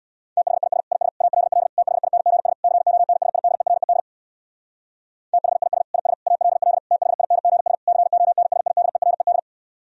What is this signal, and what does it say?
Morse code, heard as a keyed tone, about 64 wpm, 700 Hz. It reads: THIS IS ALL THINGS CONSIDERED THIS IS ALL THINGS CONSIDERED